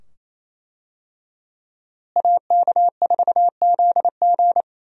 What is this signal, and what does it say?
Morse code, heard as a keyed tone, about 28 wpm, 700 Hz. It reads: AK4ZG
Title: USA Call Signs - 5 Characters - Repeated 2x - 28wpm